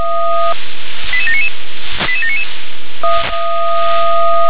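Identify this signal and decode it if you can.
Five-tone selective call (ZVEI), heard as quick runs of five tones, tones of about 70 ms. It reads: 88799 88799
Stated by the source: Received by SDR.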